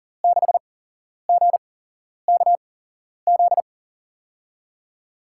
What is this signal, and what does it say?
Morse code, heard as a keyed tone, about 40 wpm, spaced as for 11 wpm, 700 Hz. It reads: 6GKZ
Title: USA Call Signs - 6 Characters - 8x Character Spacing - 40wpm